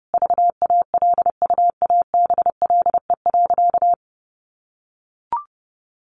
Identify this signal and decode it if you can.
Morse code, heard as a keyed tone, about 30 wpm, 700 Hz. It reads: VALUABLE.
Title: Sentences from Top 1100 Words 30wpm